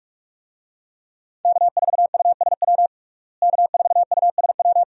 Morse code, heard as a keyed tone, about 46 wpm, 700 Hz. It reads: K4USW K4USW
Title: USA Call Signs - 5 Characters - Repeated 3x - 45wpm